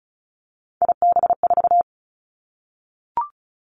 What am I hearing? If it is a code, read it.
Morse code, heard as a keyed tone, about 36 wpm, 700 Hz. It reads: IB4
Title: Three Challenging Letter-Number Combinations 35wpm